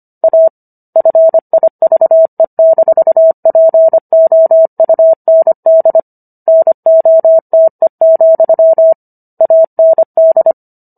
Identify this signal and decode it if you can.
Morse code, heard as a keyed tone, about 25 wpm, 650 Hz. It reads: A FIVE-POUND NOTE, AND